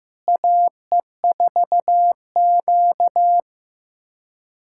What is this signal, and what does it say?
Morse code, heard as a keyed tone, about 15 wpm, 700 Hz. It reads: AE4Q
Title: USA Callsign DE Callsign - 4 Characters - 15wpm